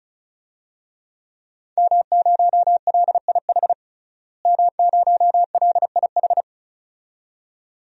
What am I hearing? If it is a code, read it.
Morse code, heard as a keyed tone, about 36 wpm, 700 Hz. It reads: M0LIH M0LIH
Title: DXCC-100 Call Signs - 5 Characters - Repeated 2x - 35wpm